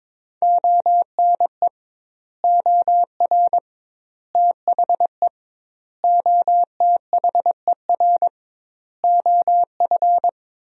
Morse code, heard as a keyed tone, about 22 wpm, 700 Hz. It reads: ONE OR THE OTHER OF
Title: 2x Word-Spacing - Sentences from Top 400 Words - 22wpm